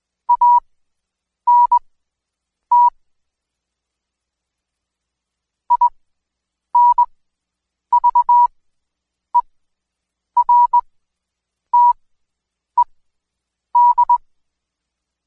Morse code, heard as a keyed tone, about 20 wpm, spaced as for 8 wpm, 970 Hz. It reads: ANT INVERTED
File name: Code Quick / AudioDownld2 / Seg114-Day19-DayDrill 19